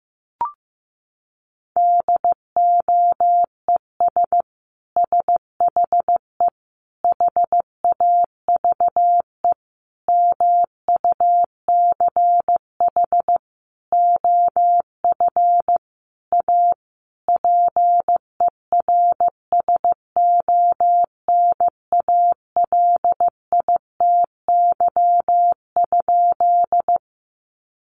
Morse code, heard as a keyed tone, about 15 wpm, 700 Hz. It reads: DOES SHE HAVE MUCH OF A PERSONALITY?
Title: Sentences from Top 1400 Words 15wpm